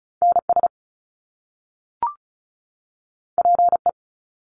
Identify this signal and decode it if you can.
Morse code, heard as a keyed tone, about 36 wpm, 700 Hz. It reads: NS PE
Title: Canadian Provinces and Territories 35wpm